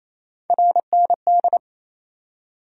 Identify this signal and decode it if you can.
Morse code, heard as a keyed tone, about 28 wpm, 700 Hz. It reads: RND
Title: USA Call Sign Postfixes - 3 Letters 28wpm